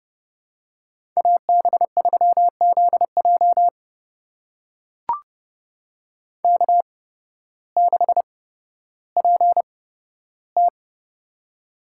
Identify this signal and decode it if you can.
Morse code, heard as a keyed tone, about 30 wpm, spaced as for 14 wpm, 700 Hz. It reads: AB3ZJ K6PT
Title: USA Call Signs - 8x Character Spacing - 30wpm